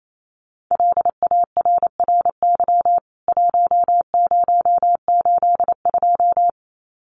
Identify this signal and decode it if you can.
Morse code, heard as a keyed tone, about 28 wpm, 700 Hz. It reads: LARRY 1082
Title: CWT Exchange 28wpm